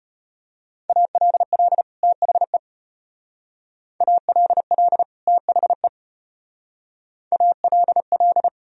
Morse code, heard as a keyed tone, about 33 wpm, 700 Hz. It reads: ALL THE ALL THE ALL